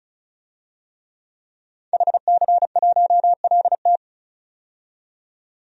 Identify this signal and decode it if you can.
Morse code, heard as a keyed tone, about 36 wpm, 700 Hz. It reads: HC1LT
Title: DXCC-100 Call Signs - 5 Characters - 35wpm